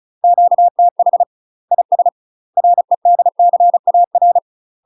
Morse code, heard as a keyed tone, about 34 wpm, 700 Hz. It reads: QTH IS REDCAR
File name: UK MASTER QSOs TRAINING / 35WPM / UK MASTER QSOs 0034